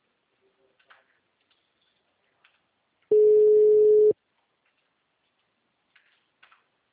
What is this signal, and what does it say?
Ringback tone (Europe, 425 Hz).
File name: out- -2008- - - .24336